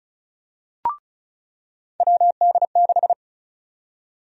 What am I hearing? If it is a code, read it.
Morse code, heard as a keyed tone, about 36 wpm, 700 Hz. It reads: WD6
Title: USA Call Sign Prefixes - 35wpm